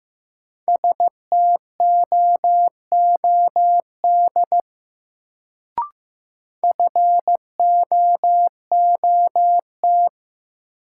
Morse code, heard as a keyed tone, about 15 wpm, 700 Hz. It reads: STOOD FOOT
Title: Sentences from Top 500 Words - New Word Drill - 15wpm